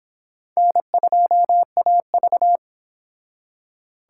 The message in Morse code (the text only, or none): N2AV